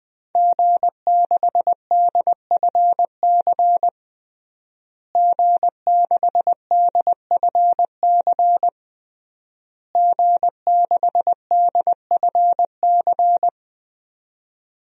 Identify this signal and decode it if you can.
Morse code, heard as a keyed tone, about 21 wpm, 700 Hz. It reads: G6DFC G6DFC G6DFC
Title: DXCC-100 Call Signs - 5 Characters - Repeated 3x - 20wpm